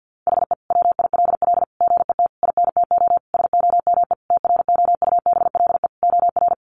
Morse code, heard as a keyed tone, about 49 wpm, 700 Hz. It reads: HE WILL GET INTO SOME TROUBLE OR
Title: Sentences from Top 1500 Words 50wpm